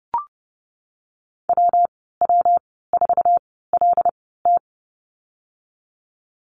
Morse code, heard as a keyed tone, about 30 wpm, spaced as for 17 wpm, 700 Hz. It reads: WW4LT
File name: Lesson 19g - Letter w - Call Signs 30wpm